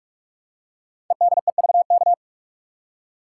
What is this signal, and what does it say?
Morse code, heard as a keyed tone, about 44 wpm, 700 Hz. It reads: EDEVK
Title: Code Groups - 5 Letters - 45wpm